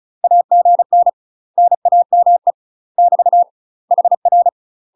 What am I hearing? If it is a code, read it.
Morse code, heard as a keyed tone, about 34 wpm, 700 Hz. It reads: AGN NAME <BT> HR
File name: UK MASTER QSOs TRAINING/35WPM